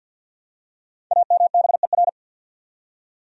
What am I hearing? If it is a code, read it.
Morse code, heard as a keyed tone, about 49 wpm, 700 Hz. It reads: AMBER
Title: USA Common Names - 50wpm